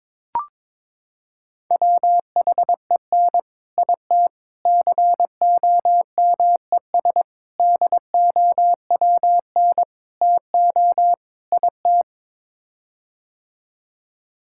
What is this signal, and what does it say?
Morse code, heard as a keyed tone, about 22 wpm, 700 Hz. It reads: WHEN IT COMES DOWN TO IT
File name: practice / 22wpm / Sets of 6 Words - 22wpm